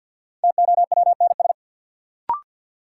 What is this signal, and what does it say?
Morse code, heard as a keyed tone, about 53 wpm, 700 Hz. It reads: TOWNS
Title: Sentences from Top 1000 Words - Review Part 6 - New Word Drill - 50wpm